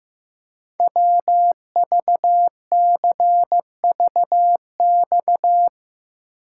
Morse code, heard as a keyed tone, about 16 wpm, 700 Hz. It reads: WVCVX
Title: Code Groups - 5 Letter:Numbers - 15wpm